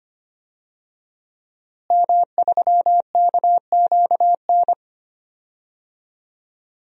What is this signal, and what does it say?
Morse code, heard as a keyed tone, about 25 wpm, 700 Hz. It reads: M3KQN
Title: DXCC-100 Call Signs - 5 Characters - Repeated 3x - 25wpm